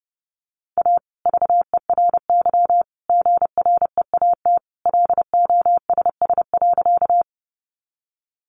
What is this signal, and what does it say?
Morse code, heard as a keyed tone, about 30 wpm, 700 Hz. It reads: A VERY GREAT LOSS.
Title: Sentences from Top 1400 Words 30wpm